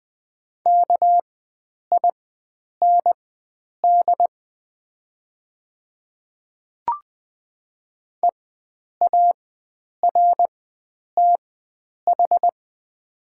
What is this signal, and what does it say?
Morse code, heard as a keyed tone, about 20 wpm, spaced as for 9 wpm, 700 Hz. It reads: KIND EARTH
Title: 4x Character Spacing - Top 500 Words - 20wpm